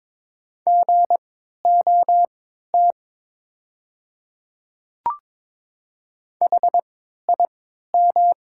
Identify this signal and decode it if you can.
Morse code, heard as a keyed tone, about 22 wpm, spaced as for 12 wpm, 700 Hz. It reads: GOT HIM
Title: 3x Character Spacing - Top 300 Words - 22wpm